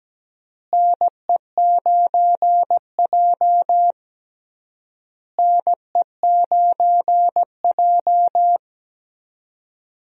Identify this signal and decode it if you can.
Morse code, heard as a keyed tone, about 17 wpm, 700 Hz. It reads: NE9J NE9J
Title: USA Call Signs - 4 Characters - Repeated 3x - 17wpm